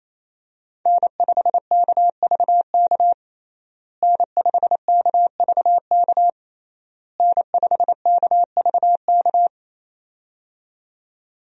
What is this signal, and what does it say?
Morse code, heard as a keyed tone, about 28 wpm, 700 Hz. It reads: N5KVK N5KVK N5KVK
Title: USA Call Signs - 5 Characters - Repeated 3x - 28wpm